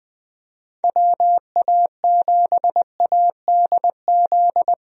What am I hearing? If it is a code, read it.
Morse code, heard as a keyed tone, about 21 wpm, 700 Hz. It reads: WA7ADZ
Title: USA Call Signs - 6 Characters - Repeated 2x - 20wpm